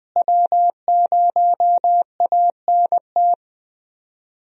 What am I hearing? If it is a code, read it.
Morse code, heard as a keyed tone, about 20 wpm, 700 Hz. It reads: W0ANT
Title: USA Call Signs - 5 Characters - Repeated 2x - 20wpm